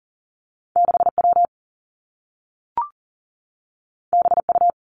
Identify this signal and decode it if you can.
Morse code, heard as a keyed tone, about 40 wpm, 700 Hz. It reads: 6 W B U
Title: Two Challenging Letter-Number Combinations 40wpm